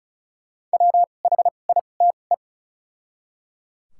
Morse code, heard as a keyed tone, about 34 wpm, spaced as for 25 wpm, 700 Hz. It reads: WHITE
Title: 2x Character Spacing - Top 300 Words - 35wpm